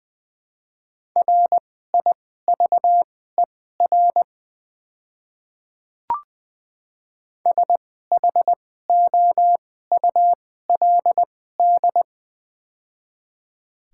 Morse code, heard as a keyed tone, about 20 wpm, spaced as for 14 wpm, 700 Hz. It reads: RIVER SHOULD
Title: 2x Character Spacing - Top 300 Words - 20wpm